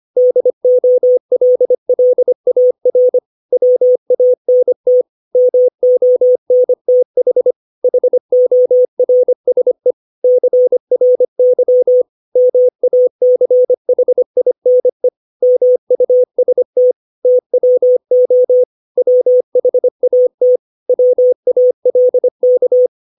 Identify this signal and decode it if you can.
Morse code, heard as a keyed tone, about 25 wpm, 500 Hz. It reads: DOLLAR WANT MONTH HORSE CRY MACHINE MUST TWO WHAT WALK
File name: 632 words / 632 Words 25WPM